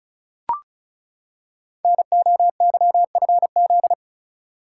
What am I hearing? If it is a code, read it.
Morse code, heard as a keyed tone, about 36 wpm, 690 Hz. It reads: NOYFZ
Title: Code Groups - 5 Letters - 35wpm